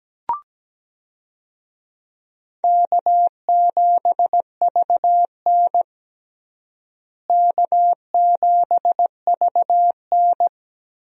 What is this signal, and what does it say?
Morse code, heard as a keyed tone, about 17 wpm, 700 Hz. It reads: K7VN K7VN
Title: USA Call Signs - 4 Characters - Repeated 3x - 17wpm